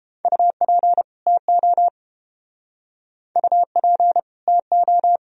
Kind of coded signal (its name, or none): Morse code